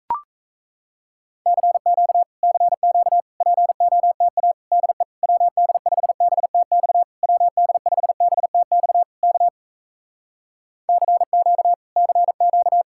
Morse code, heard as a keyed tone, about 38 wpm, 700 Hz. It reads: CQ CQ POTA DE WD5BTX WD5BTX K CQ CQ